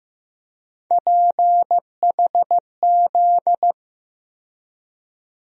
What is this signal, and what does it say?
Morse code, heard as a keyed tone, about 15 wpm, 700 Hz. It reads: PHZ
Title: USA Call Sign Postfixes - 3 Letters 15wpm